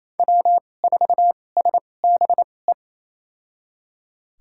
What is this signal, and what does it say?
Morse code, heard as a keyed tone, about 28 wpm, spaced as for 20 wpm, 700 Hz. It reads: W4SBE
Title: USA Call Signs - 5 Characters - 2x Character Spacing - 28wpm